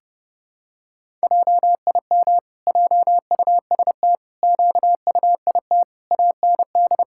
Morse code, heard as a keyed tone, about 32 wpm, 700 Hz. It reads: JIM JUST QUIT AND
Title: Pangram Challenge v2 - 30wpm